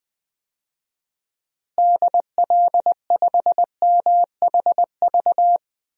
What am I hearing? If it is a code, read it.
Morse code, heard as a keyed tone, about 21 wpm, 700 Hz. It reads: DL5MHV
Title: DXCC-100 Call Signs - 6 Characters - Repeated 2x - 20wpm